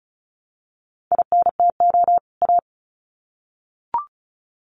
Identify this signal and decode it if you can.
Morse code, heard as a keyed tone, about 34 wpm, 700 Hz. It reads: INTO A
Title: Sets of 2 Words 35wpm